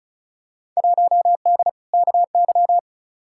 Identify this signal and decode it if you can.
Morse code, heard as a keyed tone, about 34 wpm, 700 Hz. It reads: 1D KY
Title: Field Day Exchanges - 35wpm